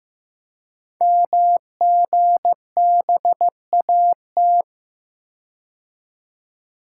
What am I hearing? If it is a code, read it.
Morse code, heard as a keyed tone, about 15 wpm, 700 Hz. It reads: MGBAT
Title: Code Groups - 5 Letter:Numbers - 15wpm